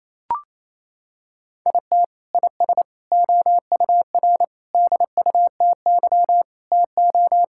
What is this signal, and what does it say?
Morse code, heard as a keyed tone, about 28 wpm, 700 Hz. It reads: IT IS OUR DUTY TO
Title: Sentences from Top 1600 Words 28wpm